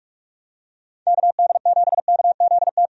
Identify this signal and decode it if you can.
Morse code, heard as a keyed tone, about 42 wpm, 700 Hz. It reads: KD7KZT
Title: USA Call Signs - 6 Characters - Repeated 2x - 45wpm